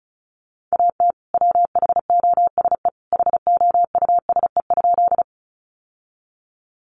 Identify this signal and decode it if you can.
Morse code, heard as a keyed tone, about 34 wpm, 700 Hz. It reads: AT WHOSE HOUSE?
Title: Sentences from Top 1200 Words 35wpm